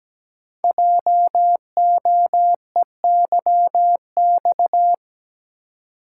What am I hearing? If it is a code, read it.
Morse code, heard as a keyed tone, about 17 wpm, 700 Hz. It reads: JOEYX
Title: Code Groups - 5 Letter:Numbers - 17wpm